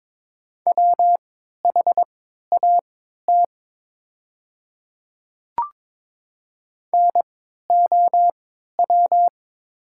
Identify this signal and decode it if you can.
Morse code, heard as a keyed tone, about 22 wpm, spaced as for 12 wpm, 700 Hz. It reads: WHAT NOW
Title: 3x Character Spacing - Top 100 Words - 22wpm